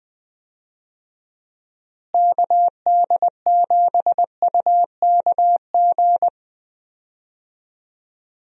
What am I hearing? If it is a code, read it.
Morse code, heard as a keyed tone, about 21 wpm, 700 Hz. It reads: KD7UKG
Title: USA Call Signs - 6 Characters - Repeated 2x - 20wpm